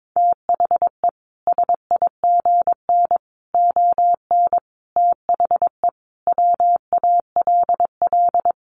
Morse code, heard as a keyed tone, about 22 wpm, 700 Hz. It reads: THE SIGN ON THE WALL
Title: A Sound Of Thunder 22wpm